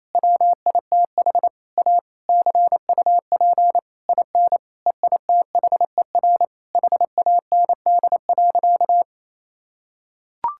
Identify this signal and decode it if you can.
Morse code, heard as a keyed tone, about 28 wpm, 700 Hz. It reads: WITH A CUP IN EITHER HAND.
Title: Sentences from Top 1600 Words 28wpm